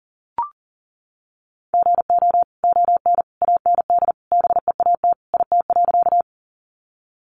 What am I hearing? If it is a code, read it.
Morse code, heard as a keyed tone, about 40 wpm, 700 Hz. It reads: GO ON AND BEAT IT.
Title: Sentences from Top 1300 Words 40wpm